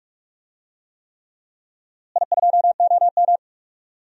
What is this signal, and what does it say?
Morse code, heard as a keyed tone, about 46 wpm, 700 Hz. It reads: I2OM
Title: DXCC-100 Call Signs - 4 Characters - Repeated 3x - 45wpm